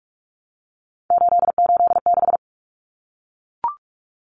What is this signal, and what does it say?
Morse code, heard as a keyed tone, about 44 wpm, 700 Hz. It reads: 886
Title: Numbers - 3-Digit 45wpm